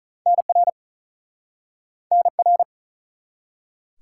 Morse code, heard as a keyed tone, about 48 wpm, 700 Hz. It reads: NR NR